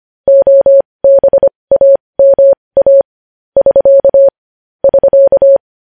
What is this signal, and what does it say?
Morse code, heard as a keyed tone, about 25 wpm, spaced as for 20 wpm, 550 Hz. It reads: OBAMA <SK> <SK>
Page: News Headlines in Morse Code at 20 WPM